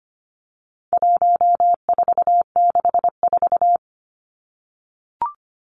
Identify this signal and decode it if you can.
Morse code, heard as a keyed tone, about 25 wpm, 700 Hz. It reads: 1464
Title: Numbers - 4-Digit 25wpm